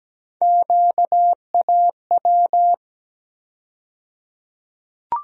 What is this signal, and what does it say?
Morse code, heard as a keyed tone, about 17 wpm, 700 Hz. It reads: QAW
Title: USA Call Sign Postfixes - 3 Letters 17wpm